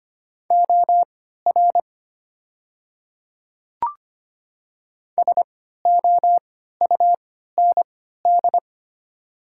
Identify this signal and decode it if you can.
Morse code, heard as a keyed tone, about 25 wpm, spaced as for 14 wpm, 700 Hz. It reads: OR SOUND
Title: 3x Character Spacing - Top 100 Words - 25wpm